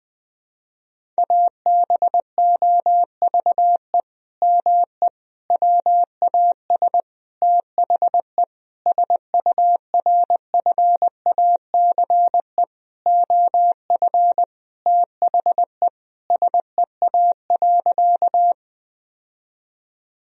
Morse code, heard as a keyed tone, about 21 wpm, 700 Hz. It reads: ABOVE ME WAS THE SURFACE OF THE SEA.
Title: Sentences from Top 1000 Words - Review Part 3 - 20wpm